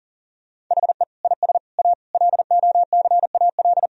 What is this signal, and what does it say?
Morse code, heard as a keyed tone, about 37 wpm, 700 Hz. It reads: HE IS A LOCAL
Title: Sentences from Top 1000 Words - Review Part 3 - 40wpm